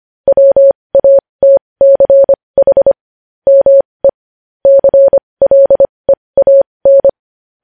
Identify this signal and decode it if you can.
Morse code, heard as a keyed tone, about 25 wpm, spaced as for 20 wpm, 550 Hz. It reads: WATCH ME CLEAN
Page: news headlines in morse code at 20 wpm